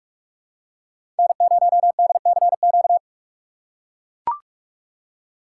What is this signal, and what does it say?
Morse code, heard as a keyed tone, about 42 wpm, 700 Hz. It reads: N0DCQ
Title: USA Call Signs - 5 Characters - 3x Character Spacing - 45wpm